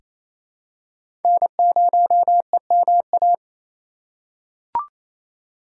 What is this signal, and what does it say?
Morse code, heard as a keyed tone, about 28 wpm, 700 Hz. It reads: N0EMA
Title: USA Call Signs - 5 Characters - 2x Character Spacing - 28wpm